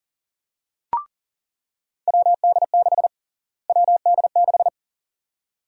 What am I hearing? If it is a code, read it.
Morse code, heard as a keyed tone, about 40 wpm, 700 Hz. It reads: WD6 WD6
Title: USA Call Sign Prefixes - Repeated 2x - 40wpm